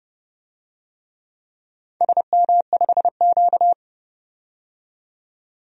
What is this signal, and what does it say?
Morse code, heard as a keyed tone, about 28 wpm, 700 Hz. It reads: SM5Q
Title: DXCC-100 Call Signs - 4 Characters - 30wpm